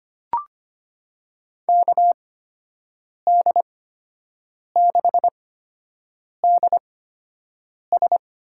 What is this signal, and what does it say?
Morse code, heard as a keyed tone, about 25 wpm, spaced as for 7 wpm, 700 Hz. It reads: KD6DS
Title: USA Call Signs - 6 Characters - 8x Character Spacing - 25wpm